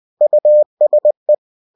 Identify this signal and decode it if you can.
Morse code, heard as a keyed tone, about 20 wpm, 600 Hz. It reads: USE
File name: cw-books / winnie-the-pooh / winnie-20wpm-page-0112